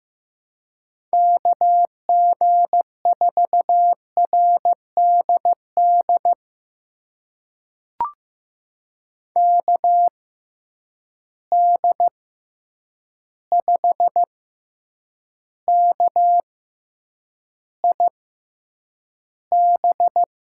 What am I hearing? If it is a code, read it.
Morse code, heard as a keyed tone, about 15 wpm, spaced as for 8 wpm, 700 Hz. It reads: KG4RDD KD5KIB
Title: USA Call Signs - 6 Characters - 6x Character Spacing - 15wpm